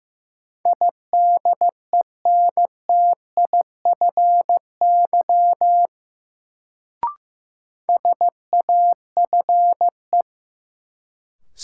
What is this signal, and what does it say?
Morse code, heard as a keyed tone, about 15 wpm, 700 Hz. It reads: IDENTIFY SAFE
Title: Sentences from Top 900 Words - New Word Drill - 15wpm